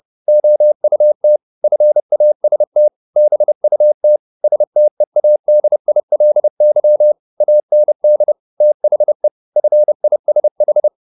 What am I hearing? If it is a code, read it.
Morse code, heard as a keyed tone, about 30 wpm, 600 Hz. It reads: OUT FAST BUT STEADILY AND THE FISH
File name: the old man and the sea / the old man and the sea 1 30wpm